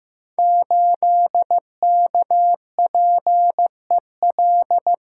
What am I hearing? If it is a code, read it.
Morse code, heard as a keyed tone, about 15 wpm, 700 Hz. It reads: 8KPEL
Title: Code Groups - 5 Letter:Numbers - 15wpm